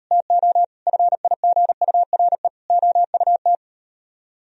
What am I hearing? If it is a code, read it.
Morse code, heard as a keyed tone, about 38 wpm, 700 Hz. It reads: TO FIGURE OUT